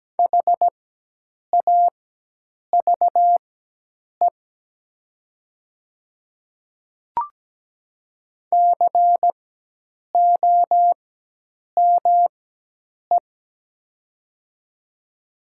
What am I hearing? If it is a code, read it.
Morse code, heard as a keyed tone, about 17 wpm, spaced as for 8 wpm, 700 Hz. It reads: HAVE COME
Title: 4x Character Spacing - Top 100 Words - 17wpm